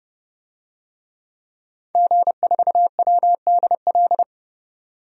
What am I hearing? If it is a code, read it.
Morse code, heard as a keyed tone, about 31 wpm, 700 Hz. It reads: G4WDL
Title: DXCC-100 Call Signs - 5 Characters - 30wpm